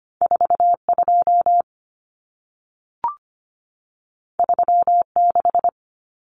Morse code, heard as a keyed tone, about 25 wpm, 700 Hz. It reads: 42 36
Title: Common Cut Numbers - 2-Digit 25wpm